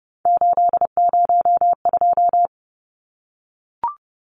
Morse code, heard as a keyed tone, about 30 wpm, 700 Hz. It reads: 802
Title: Numbers - 3-Digit 30wpm